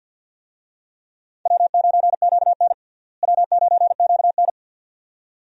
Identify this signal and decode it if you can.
Morse code, heard as a keyed tone, about 49 wpm, 700 Hz. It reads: W9QN W9QN
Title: USA Call Signs - 4 Characters - Repeated 2x - 50wpm